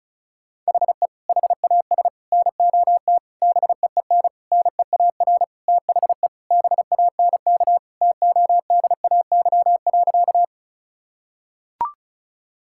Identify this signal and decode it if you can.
Morse code, heard as a keyed tone, about 36 wpm, 700 Hz. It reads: HE HAS NOT BEEN NEAR THE BANK TODAY.
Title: Sentences from Top 1000 Words - Review Part 6 - 35wpm